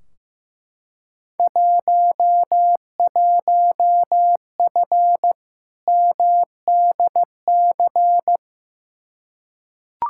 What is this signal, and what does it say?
Morse code, heard as a keyed tone, about 15 wpm, 700 Hz. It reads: 11F MDC
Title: Field Day Exchanges Encore - 15wpm